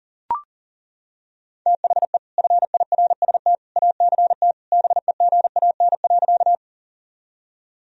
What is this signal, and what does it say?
Morse code, heard as a keyed tone, about 40 wpm, 700 Hz. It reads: THE FIRST ACT BEGAN.
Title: Sentences from Top 1000 Words - Review Part 1 - 40wpm